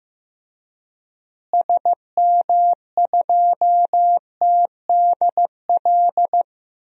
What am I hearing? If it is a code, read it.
Morse code, heard as a keyed tone, about 15 wpm, 700 Hz. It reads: SM2TDL
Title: DXCC-100 Call Signs - 6 Characters - Repeated 3x - 15wpm